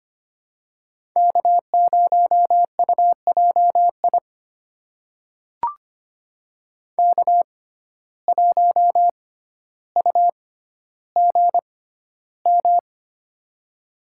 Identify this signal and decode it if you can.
Morse code, heard as a keyed tone, about 25 wpm, spaced as for 13 wpm, 700 Hz. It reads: K0UJI K1UGM
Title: USA Call Signs - 5 Characters - 6x Character Spacing - 25wpm